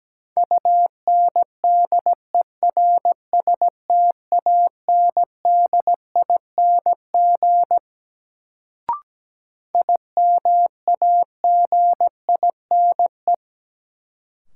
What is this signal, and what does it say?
Morse code, heard as a keyed tone, about 17 wpm, 700 Hz. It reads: UNDERSTANDING IMAGINE
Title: Sentences from Top 1400 Words - New Word Drill Alt - 17wpm